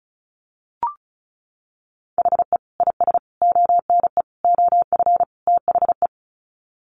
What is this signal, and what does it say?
Morse code, heard as a keyed tone, about 34 wpm, 700 Hz. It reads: HE IS ONE OF THE
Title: Sets of 5 Words Encore 35wpm